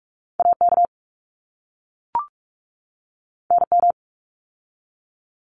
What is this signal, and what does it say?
Morse code, heard as a keyed tone, about 44 wpm, 700 Hz. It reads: AK NM
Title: ARRL Sections 45wpm